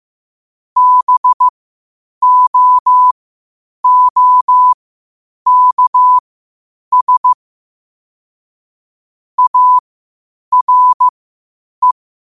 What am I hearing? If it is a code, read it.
Morse code, heard as a keyed tone, about 15 wpm, spaced as for 8 wpm, 1000 Hz. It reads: BOOKS ARE
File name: qotdfiles / 8wpm